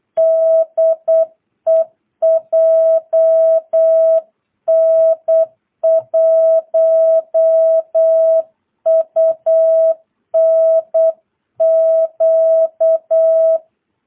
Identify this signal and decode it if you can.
Morse code, heard as a keyed tone, about 8 wpm, 650 Hz. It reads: DE JN1UNQ
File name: jh8cbh / 2022cw